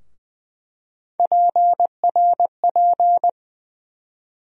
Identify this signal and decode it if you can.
Morse code, heard as a keyed tone, about 20 wpm, 700 Hz. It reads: PRP
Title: USA Call Sign Postfixes - 3 Letters 20wpm